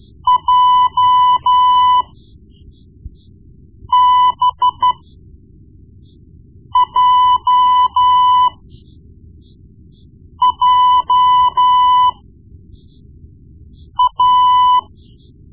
Morse code, heard as a keyed tone, about 13 wpm, spaced as for 4 wpm, 1000 Hz. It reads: JBJJA
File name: Morse Lessons / Morse Lessons / Tape-1B